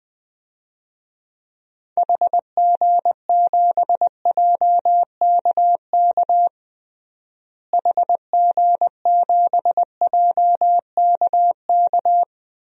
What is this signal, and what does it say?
Morse code, heard as a keyed tone, about 20 wpm, 700 Hz. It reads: HG7JKK HG7JKK
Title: DXCC-100 Call Signs - 6 Characters - Repeated 3x - 20wpm